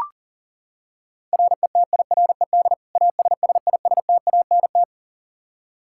Morse code, heard as a keyed tone, about 40 wpm, 700 Hz. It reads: RETIRED ASSISTANT
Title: Practice QSO - Exchange 3 - More Info - 40wpm